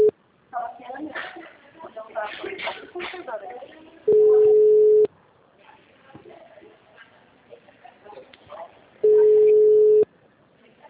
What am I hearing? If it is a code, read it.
Ringback tone (Europe, 425 Hz), with speech.
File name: out- -2016- - - .8880